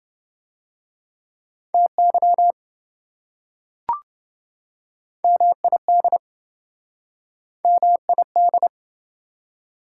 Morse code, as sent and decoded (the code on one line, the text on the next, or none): - -.-- / -- .. -.. / -- .. -..
TY MID MID